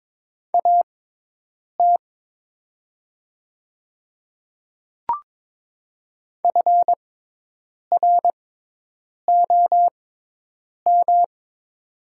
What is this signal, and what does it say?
Morse code, heard as a keyed tone, about 22 wpm, spaced as for 8 wpm, 700 Hz. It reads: AT FROM
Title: 6x Character Spacing - Top 100 Words - 22wpm